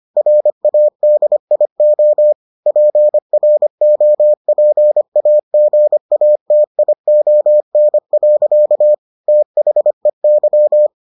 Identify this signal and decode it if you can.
Morse code, heard as a keyed tone, about 25 wpm, 600 Hz. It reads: RADIO PROPAGATION. THEY